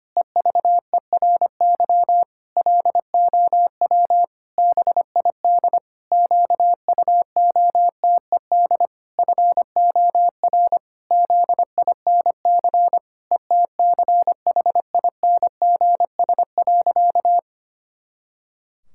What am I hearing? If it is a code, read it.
Morse code, heard as a keyed tone, about 25 wpm, 700 Hz. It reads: EVERY LOW BID QUOTED FOR ZINC ETCHINGS.
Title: Pangram Challenge v2 - 25wpm